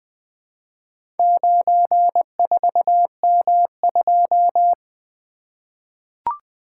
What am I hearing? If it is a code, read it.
Morse code, heard as a keyed tone, about 20 wpm, 700 Hz. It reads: 94M2
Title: Four Letter-Number Combinations - 20wpm